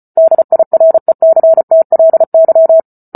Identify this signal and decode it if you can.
Morse code, heard as a keyed tone, about 34 wpm, 650 Hz. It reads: DIRECTLY
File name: Book Two / 08-Chapter 8 - 34 WPM